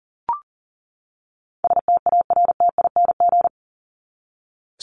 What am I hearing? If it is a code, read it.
Morse code, heard as a keyed tone, about 40 wpm, 700 Hz. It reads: STARTING
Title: ING Suffix - Top 100 Words with Base Words 40wpm